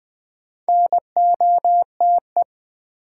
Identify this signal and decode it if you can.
Morse code, heard as a keyed tone, about 20 wpm, 700 Hz. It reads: NOTE
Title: Sets of 5 Words - New Word Drill - 20wpm